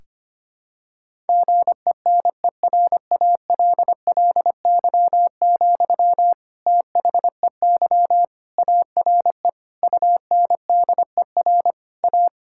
Morse code, heard as a keyed tone, about 25 wpm, 700 Hz. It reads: GENERALLY, THEY ARE UNDER A
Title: Sentences from Top 1000 Words - Review Part 6 - 25wpm